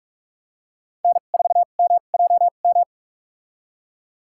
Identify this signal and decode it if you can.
Morse code, heard as a keyed tone, about 42 wpm, spaced as for 31 wpm, 700 Hz. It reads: N4MJM
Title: USA Call Signs - 5 Characters - 2x Character Spacing - 45wpm